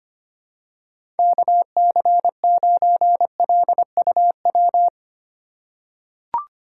Morse code, heard as a keyed tone, about 25 wpm, 700 Hz. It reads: KC9LUW
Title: USA Call Signs - 6 Characters - 3x Character Spacing - 25wpm